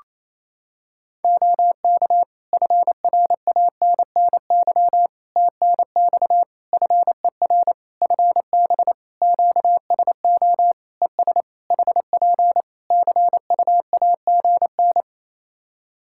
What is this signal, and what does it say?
Morse code, heard as a keyed tone, about 28 wpm, 700 Hz. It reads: OK FRANNY TNX FER FB QSO ES HP CUAGN
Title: Practice QSO - Exchange 4 - Ending - 28wpm